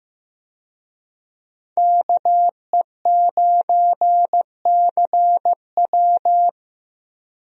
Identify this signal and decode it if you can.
Morse code, heard as a keyed tone, about 15 wpm, 700 Hz. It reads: KE9CW
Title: USA Call Signs - 5 Characters - Repeated 3x - 15wpm